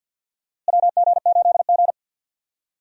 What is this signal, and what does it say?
Morse code, heard as a keyed tone, about 52 wpm, 700 Hz. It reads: WG8G
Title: USA Call Signs - 4 Characters - Repeated 2x - 50wpm